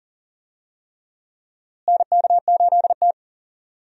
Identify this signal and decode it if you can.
Morse code, heard as a keyed tone, about 40 wpm, 700 Hz. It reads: NK8T
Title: USA Call Signs - 4 Characters - Repeated 3x - 40wpm